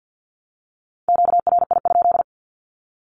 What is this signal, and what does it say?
Morse code, heard as a keyed tone, about 50 wpm, 700 Hz. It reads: QRI?
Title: Q-Codes 50wpm